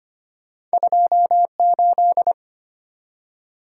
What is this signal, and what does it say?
Morse code, heard as a keyed tone, about 25 wpm, 700 Hz. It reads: 28
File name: Two Letter-Number Combinations - 25wpm